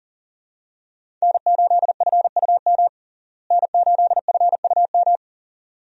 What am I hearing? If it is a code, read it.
Morse code, heard as a keyed tone, about 40 wpm, 700 Hz. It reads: N8FUM N8FUM
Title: USA Call Signs - 5 Characters - Repeated 2x - 40wpm